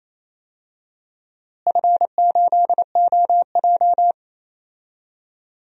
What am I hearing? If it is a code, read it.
Morse code, heard as a keyed tone, about 28 wpm, 700 Hz. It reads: F8OJ
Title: DXCC-100 Call Signs - 4 Characters - Repeated 3x - 28wpm